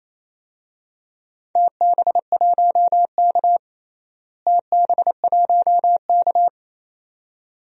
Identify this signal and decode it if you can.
Morse code, heard as a keyed tone, about 28 wpm, 700 Hz. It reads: TB1K TB1K
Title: DXCC-100 Call Signs - 4 Characters - Repeated 2x - 28wpm